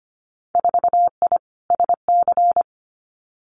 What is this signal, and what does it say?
Morse code, heard as a keyed tone, about 25 wpm, 690 Hz. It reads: 4I SC
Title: Winter Field Day Exchanges 25wpm